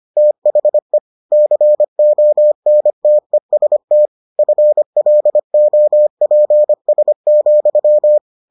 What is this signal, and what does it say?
Morse code, heard as a keyed tone, about 25 wpm, 600 Hz. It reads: THE CONTEST FLOPS,